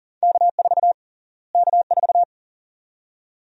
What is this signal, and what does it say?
Morse code, heard as a keyed tone, about 37 wpm, 700 Hz. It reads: K4 K4
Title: USA Call Sign Prefixes - Repeated 2x - 40wpm